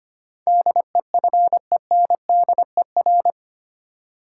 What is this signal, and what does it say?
Morse code, heard as a keyed tone, about 25 wpm, 690 Hz. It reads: DEFENDER
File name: 8 Letter Words - First 3 Letters Given - 25wpm